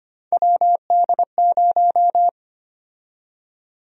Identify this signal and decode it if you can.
Morse code, heard as a keyed tone, about 25 wpm, 700 Hz. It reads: WD0
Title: USA Call Sign Prefixes - 25wpm